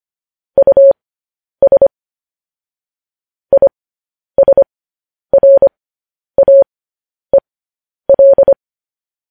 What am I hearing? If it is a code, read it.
Morse code, heard as a keyed tone, about 25 wpm, spaced as for 10 wpm, 550 Hz. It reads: US ISRAEL
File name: podcast / News headlines-1x-25x10